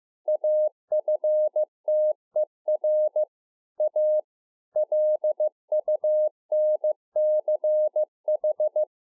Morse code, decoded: AFTER A LUNCH

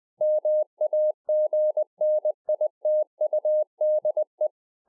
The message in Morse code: MAGNITUDE

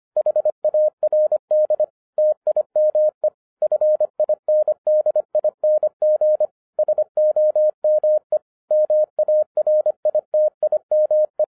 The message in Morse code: HARD TIME FINDING SOME MARITIME

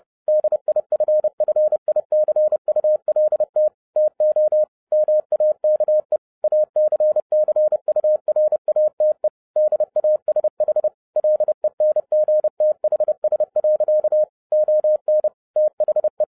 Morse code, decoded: DIFFICULT TO MAKE ACCURATE DASH LENGTHS. ON THE

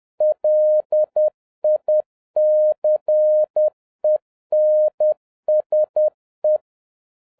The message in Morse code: LICENSE

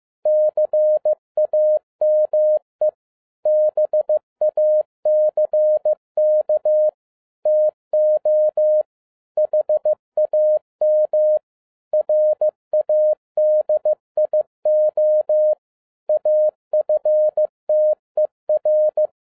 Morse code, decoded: CAME BACK TO HAM RADIO AFTER

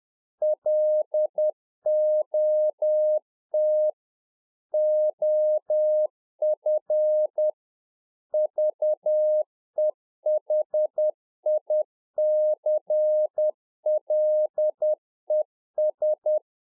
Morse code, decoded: LOT OF VEHICLES

